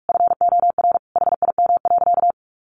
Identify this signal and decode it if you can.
Morse code, heard as a keyed tone, about 44 wpm, 700 Hz. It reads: FOR HIM.
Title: Sentences from Top 1100 Words 45wpm